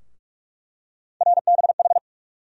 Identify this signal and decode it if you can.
Morse code, heard as a keyed tone, about 48 wpm, 700 Hz. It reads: RBH
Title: USA Call Sign Postfixes - 3 Letters 45wpm